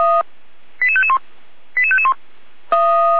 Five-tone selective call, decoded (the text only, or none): 88551 88551